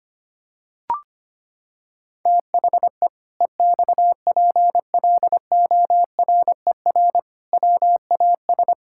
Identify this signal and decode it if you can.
Morse code, heard as a keyed tone, about 25 wpm, 700 Hz. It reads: THE EXPLORER WAS
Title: Pangram Challenge v2 - 25wpm